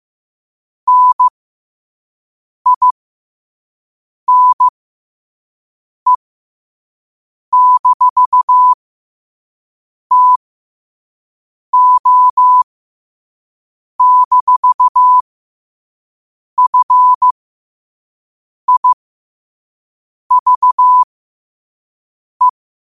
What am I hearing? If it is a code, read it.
Morse code, heard as a keyed tone, about 15 wpm, spaced as for 5 wpm, 1000 Hz. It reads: NINE-TO-FIVE
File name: qotdfiles / 5wpm